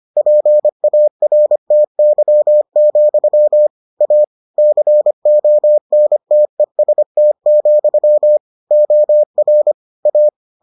Morse code, heard as a keyed tone, about 25 wpm, 600 Hz. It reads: PARTY, A CONTEST, OR A